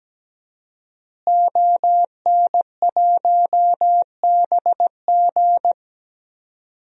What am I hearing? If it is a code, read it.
Morse code, heard as a keyed tone, about 17 wpm, 700 Hz. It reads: ON1BG
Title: DXCC-100 Call Signs - 5 Characters - Repeated 3x - 17wpm